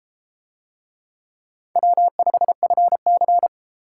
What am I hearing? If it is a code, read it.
Morse code, heard as a keyed tone, about 30 wpm, 700 Hz. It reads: W5FC